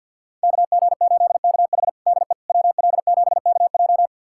Morse code, heard as a keyed tone, about 47 wpm, 700 Hz. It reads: KG8KH DE WL7KJ
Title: USA Callsign DE Callsign - 5 Characters - 50wpm